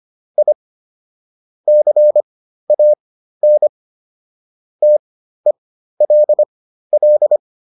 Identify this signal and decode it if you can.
Morse code, heard as a keyed tone, about 25 wpm, spaced as for 13 wpm, 600 Hz. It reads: I CAN TELL